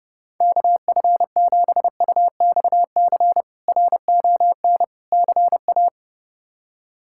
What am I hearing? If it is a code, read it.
Morse code, heard as a keyed tone, about 32 wpm, 700 Hz. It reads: KF7UXC RON CA
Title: K1USN SST QSO Practice - 30wpm